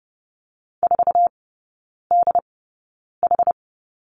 Morse code, heard as a keyed tone, about 30 wpm, spaced as for 9 wpm, 700 Hz. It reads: 4DH
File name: course / 30:10wpm / 087 Lesson 15e - Number 5 - Three Characters 30wpm